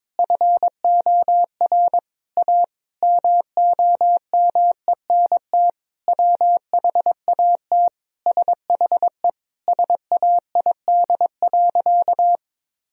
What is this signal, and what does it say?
Morse code, heard as a keyed tone, about 22 wpm, 700 Hz. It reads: FOR A MOMENT WHAT SHE SAID.
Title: Sentences from Top 1000 Words - Review Part 4 - 22wpm